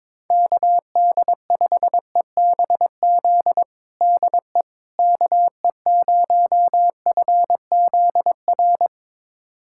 Morse code, heard as a keyed tone, about 22 wpm, 700 Hz. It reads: KD5EBZ DE KE0FZR
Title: USA Callsign DE Callsign - 6 Characters - 22wpm